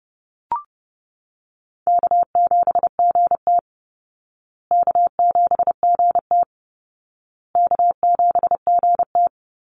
Morse code, heard as a keyed tone, about 30 wpm, 700 Hz. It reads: K7GT K7GT K7GT
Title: Call Signs Repeated 2x - 30wpm